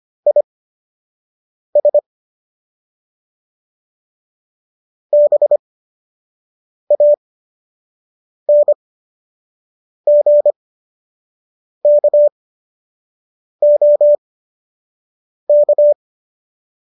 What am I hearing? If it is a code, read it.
Morse code, heard as a keyed tone, about 25 wpm, spaced as for 6 wpm, 590 Hz. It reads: IS BANGKOK